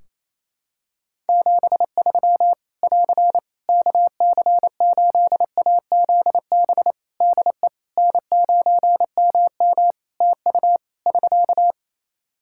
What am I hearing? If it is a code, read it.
Morse code, heard as a keyed tone, about 27 wpm, 700 Hz. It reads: 73 <AR> KC8AZB DE N9MM TU <SK>
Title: K1USN SST QSO Practice 28wpm